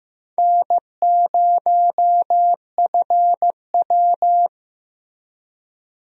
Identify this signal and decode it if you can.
Morse code, heard as a keyed tone, about 15 wpm, 700 Hz. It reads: N0FW
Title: USA Call Signs - 4 Characters - Repeated 2x - 15wpm